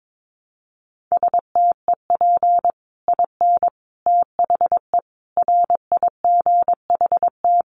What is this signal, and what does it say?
Morse code, heard as a keyed tone, about 22 wpm, 700 Hz. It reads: STEP IN THE RIGHT
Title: Sets of 5 Words Encore 22wpm